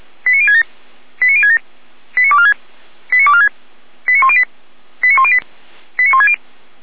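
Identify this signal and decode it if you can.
Five-tone selective call (ZVEI), heard as quick runs of five tones, tones of about 70 ms. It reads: 89067 89067 89246 89246 89108 89108 89160
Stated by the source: Five-tone ZVEI selective call received by SDR, from the feuerwehr-orlamuende.org fire brigade alerting archive.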